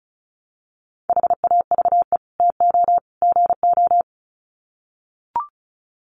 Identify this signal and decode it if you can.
Morse code, heard as a keyed tone, about 34 wpm, 700 Hz. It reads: HAVE TO GO
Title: Sets of 3 Words 35wpm